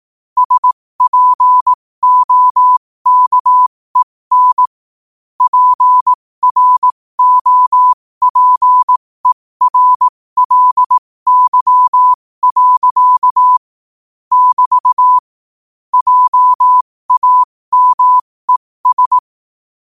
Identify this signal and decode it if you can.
Morse code, heard as a keyed tone, about 18 wpm, spaced as for 15 wpm, 1000 Hz. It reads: SPOKEN PROPERLY. <BT> JAMES
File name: qotdfiles / 15wpm